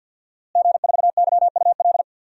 Morse code, heard as a keyed tone, about 47 wpm, 700 Hz. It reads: G4YUL
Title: DXCC-100 Call Signs - 5 Characters - Repeated 3x - 50wpm